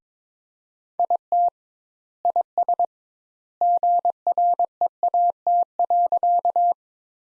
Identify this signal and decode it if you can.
Morse code, heard as a keyed tone, about 22 wpm, 700 Hz. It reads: IT IS GREAT.
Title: Sentences Top 200 Words - 22wpm